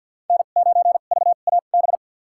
Morse code, heard as a keyed tone, about 51 wpm, spaced as for 36 wpm, 700 Hz. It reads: N9VAB
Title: USA Call Signs - 5 Characters - 2x Character Spacing - 50wpm